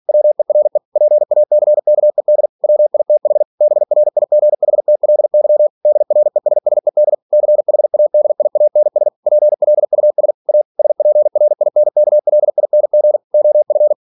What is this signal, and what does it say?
Morse code, heard as a keyed tone, about 46 wpm, 600 Hz. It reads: WERE PACKED WITH BRIGHTLY DRESSED CHADIANS PLUS A SPRINKLING OF
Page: Never 47wpm